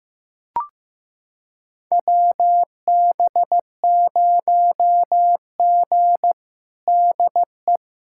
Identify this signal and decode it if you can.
Morse code, heard as a keyed tone, about 16 wpm, 700 Hz. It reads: WB0G DE
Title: USA Callsign DE Callsign - 4 Characters - 15wpm